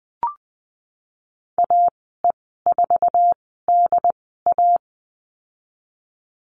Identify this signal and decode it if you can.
Morse code, heard as a keyed tone, about 20 wpm, spaced as for 15 wpm, 700 Hz. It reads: AE4DA
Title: 069 Lesson 12g - Letter d - Call Signs 20wpm